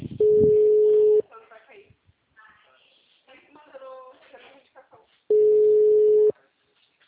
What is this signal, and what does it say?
Ringback tone (Europe, 425 Hz), with speech.